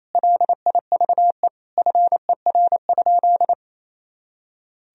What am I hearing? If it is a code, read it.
Morse code, heard as a keyed tone, about 28 wpm, 700 Hz. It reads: LIVE FER?
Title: Sentences from Top 200 Words and Abbreviations 28wpm